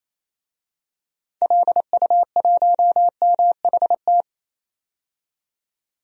Morse code, heard as a keyed tone, about 28 wpm, 700 Hz. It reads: LU1MHT
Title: DXCC-100 Call Signs - 6 Characters - 28wpm